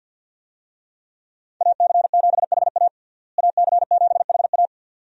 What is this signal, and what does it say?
Morse code, heard as a keyed tone, about 52 wpm, 700 Hz. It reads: AC7HA AC7HA
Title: USA Call Signs - 5 Characters - Repeated 2x - 50wpm